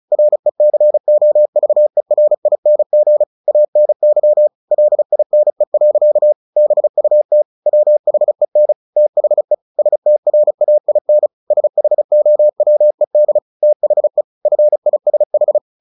Morse code, heard as a keyed tone, about 34 wpm, 600 Hz. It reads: RECOVERING ANY LINE. BUT WHEN THE STRAIN SHOWED THE FISH
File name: the old man and the sea 2 35wpm